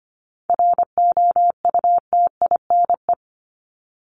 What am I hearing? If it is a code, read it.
Morse code, heard as a keyed tone, about 25 wpm, 700 Hz. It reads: ROUTINE
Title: Top 1501-1750 Words 25wpm